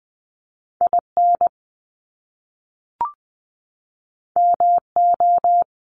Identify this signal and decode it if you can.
Morse code, heard as a keyed tone, about 20 wpm, 700 Hz. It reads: IN MO
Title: US States 20wpm